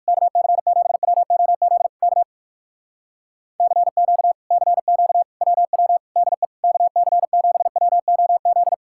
Morse code, heard as a keyed tone, about 66 wpm, spaced as for 56 wpm, 700 Hz. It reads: KC7WOZ K CQ CQ WW DE KC7WOZ
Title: Speed-Racing Common QSO Phrases - 45wpm